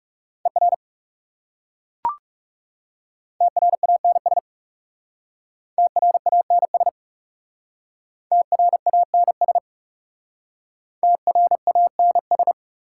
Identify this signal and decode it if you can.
Morse code, heard as a keyed tone, about 41 wpm, 690 Hz. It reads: ER TRANS TRANS TRANS TRANS